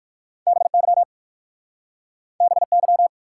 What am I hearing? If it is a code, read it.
Morse code, heard as a keyed tone, about 60 wpm, 700 Hz. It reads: BY BY